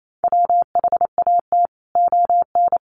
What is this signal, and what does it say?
Morse code, heard as a keyed tone, about 28 wpm, 700 Hz. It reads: WHAT ON
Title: Sentences from Top 700 Words 28wpm